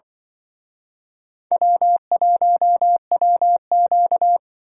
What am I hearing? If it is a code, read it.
Morse code, heard as a keyed tone, about 25 wpm, 700 Hz. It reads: W1WQ
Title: Speed Racing - USA 1x2 Call Signs 20wpm